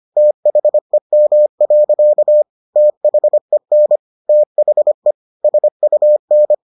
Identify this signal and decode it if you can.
Morse code, heard as a keyed tone, about 25 wpm, 600 Hz. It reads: THEM. THEN THE SUN